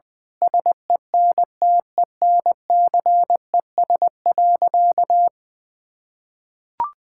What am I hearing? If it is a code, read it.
Morse code, heard as a keyed tone, about 20 wpm, 700 Hz. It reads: SENTENCES.
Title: Pangram Challenge v2 - 20wpm